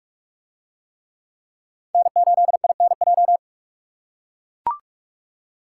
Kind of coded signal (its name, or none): Morse code